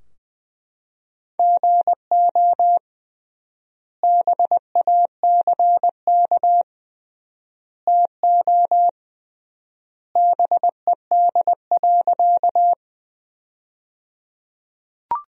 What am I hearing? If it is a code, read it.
Morse code, heard as a keyed tone, about 20 wpm, 700 Hz. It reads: GO BACK TO BED.
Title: 3x Word-Spacing - Sentences from Top 500 Words - 20wpm